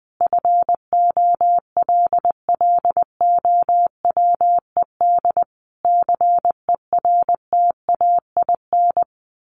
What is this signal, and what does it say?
Morse code, heard as a keyed tone, about 20 wpm, 700 Hz. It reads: FOLLOWED CERTAIN
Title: A Sound Of Thunder 20wpm